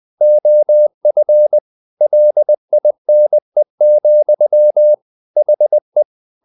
Morse code, heard as a keyed tone, about 20 wpm, 600 Hz. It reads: OF LINE, HE
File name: the old man and the sea part 2 20wpm